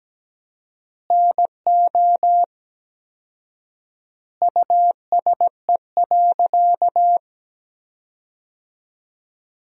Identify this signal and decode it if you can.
Morse code, heard as a keyed tone, about 17 wpm, 700 Hz. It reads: NO USE.
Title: 4x Word-Spacing - Sentences from Top 100 Words - 17wpm